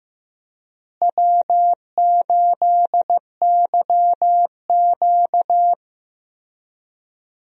Morse code, heard as a keyed tone, about 15 wpm, 700 Hz. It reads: W8YQ
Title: USA Call Signs - 4 Characters - Repeated 3x - 15wpm